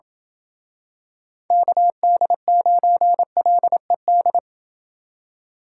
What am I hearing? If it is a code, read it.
Morse code, heard as a keyed tone, about 28 wpm, 700 Hz. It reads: KD9LED